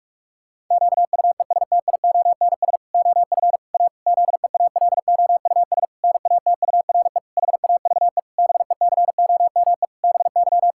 Morse code, heard as a keyed tone, about 46 wpm, 700 Hz. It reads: QUESTIONS OF A ZEALOUS NATURE HAVE BECOME BY
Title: Pangram Challenge v2 - 45wpm